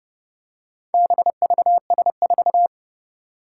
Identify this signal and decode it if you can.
Morse code, heard as a keyed tone, about 31 wpm, 700 Hz. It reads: BVS4
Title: Four Challenging Letter-Number Combinations 30wpm